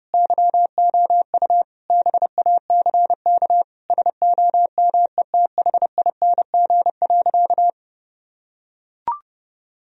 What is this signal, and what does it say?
Morse code, heard as a keyed tone, about 32 wpm, 700 Hz. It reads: YOU BACK SOMETHING.
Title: Sentences from Top 1000 Words - Review Part 3 - 30wpm